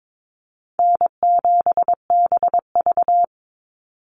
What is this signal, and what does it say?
Morse code, heard as a keyed tone, about 22 wpm, 700 Hz. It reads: N7BV
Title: Easy Call Signs Repeated 3x - 22wpm